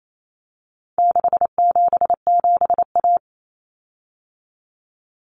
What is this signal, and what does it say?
Morse code, heard as a keyed tone, about 28 wpm, 700 Hz. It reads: 677A
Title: Common Cut Numbers - 4-Digit 28wpm